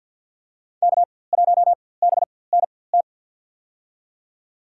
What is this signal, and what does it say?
Morse code, heard as a keyed tone, about 51 wpm, spaced as for 23 wpm, 700 Hz. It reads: K1BNT
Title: USA Call Signs - 5 Characters - 4x Character Spacing - 50wpm